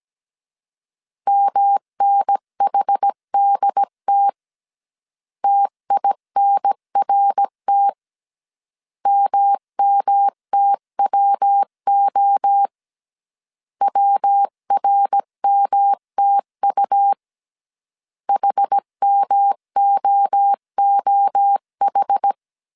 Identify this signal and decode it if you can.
Morse code, heard as a keyed tone, about 17 wpm, 800 Hz. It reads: MNHDT TINRT MMTWO WRMTU HMOOH